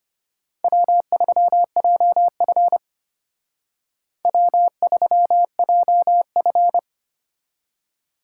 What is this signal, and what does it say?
Morse code, heard as a keyed tone, about 32 wpm, 700 Hz. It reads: W3JF W3JF